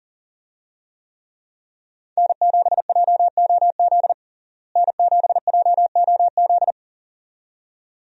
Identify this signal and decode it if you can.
Morse code, heard as a keyed tone, about 40 wpm, 700 Hz. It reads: N7JOZ N7JOZ
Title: USA Call Signs - 5 Characters - Repeated 2x - 40wpm